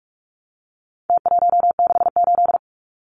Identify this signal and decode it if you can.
Morse code, heard as a keyed tone, about 44 wpm, 700 Hz. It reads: T168
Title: Numbers - Contest Style - 4-Digit 45wpm